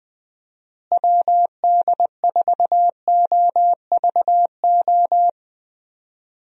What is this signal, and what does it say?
Morse code, heard as a keyed tone, about 20 wpm, 700 Hz. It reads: WD4OVO
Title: USA Call Signs - 6 Characters - Repeated 2x - 20wpm